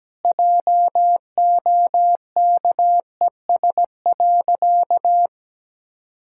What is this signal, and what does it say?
Morse code, heard as a keyed tone, about 17 wpm, 700 Hz. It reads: JOKES.
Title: Pangram Challenge v2 - 17wpm